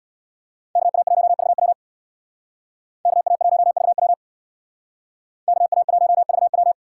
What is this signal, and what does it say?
Morse code, heard as a keyed tone, about 63 wpm, 700 Hz. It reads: KA1VW KA1VW KA1VW